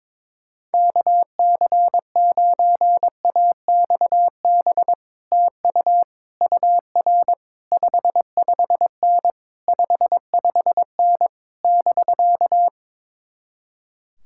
Morse code, heard as a keyed tone, about 22 wpm, 700 Hz. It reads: KC9AXB TU UR 55N 55N <BK>
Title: Speed-Racing Common QSO Phrases Encore - 22wpm